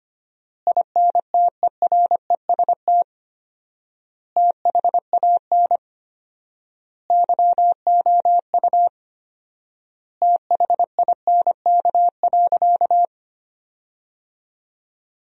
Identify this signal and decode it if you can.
Morse code, heard as a keyed tone, about 25 wpm, 700 Hz. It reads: INTEREST THAN YOU THINK.
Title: 4x Word-Spacing - Sentences from Top 400 Words - 25wpm